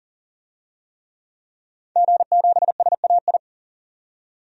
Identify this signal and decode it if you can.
Morse code, heard as a keyed tone, about 40 wpm, 700 Hz. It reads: G7SAI
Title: DXCC-100 Call Signs - 5 Characters - 40wpm